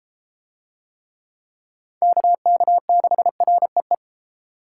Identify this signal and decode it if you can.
Morse code, heard as a keyed tone, about 34 wpm, 700 Hz. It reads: KK6REE